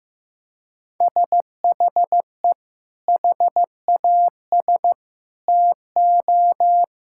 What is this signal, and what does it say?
Morse code, heard as a keyed tone, about 15 wpm, 700 Hz. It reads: SHE HAS TO
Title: Sentences from Top 1600 Words 15wpm